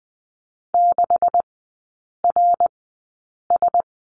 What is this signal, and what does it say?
Morse code, heard as a keyed tone, about 20 wpm, spaced as for 8 wpm, 700 Hz. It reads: 6RS
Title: 138 Lesson 23e - Letter f - Three Characters 20wpm